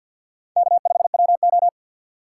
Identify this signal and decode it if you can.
Morse code, heard as a keyed tone, about 46 wpm, 700 Hz. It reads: K5WO
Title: USA Call Signs - 4 Characters - 2x Character Spacing - 50wpm